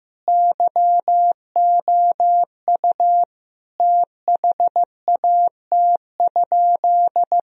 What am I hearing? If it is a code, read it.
Morse code, heard as a keyed tone, about 14 wpm, 700 Hz. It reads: YOU THAT?
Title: Sentences from Top 1000 Words - Review Part 2 - 15wpm